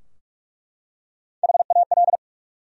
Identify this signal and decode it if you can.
Morse code, heard as a keyed tone, about 42 wpm, 700 Hz. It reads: HAL
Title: USA Common Names - Max 3 Letters - 45wpm